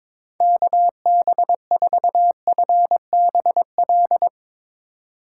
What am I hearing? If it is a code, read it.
Morse code, heard as a keyed tone, about 22 wpm, 700 Hz. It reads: KB4FBL
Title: USA Call Signs - 6 Characters - 4x Character Spacing - 22wpm